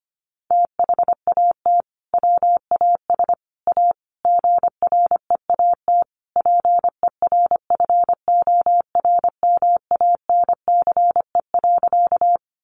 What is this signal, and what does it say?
Morse code, heard as a keyed tone, about 25 wpm, 700 Hz. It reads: THAT WAS A GREAT PERFORMANCE.
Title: Sentences from Top 1000 Words 25wpm